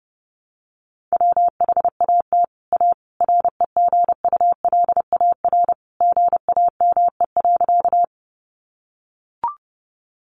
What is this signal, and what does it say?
Morse code, heard as a keyed tone, about 30 wpm, 700 Hz. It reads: WHAT A REGULAR GAME.
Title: Sentences from Top 900 Words 30wpm